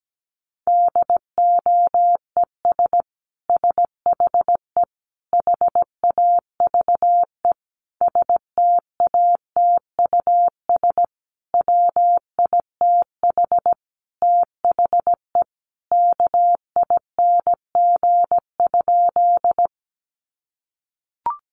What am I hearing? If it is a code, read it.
Morse code, heard as a keyed tone, about 17 wpm, 700 Hz. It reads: DOES SHE HAVE STATUS WITH THE KING?
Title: Sentences from Top 1400 Words 17wpm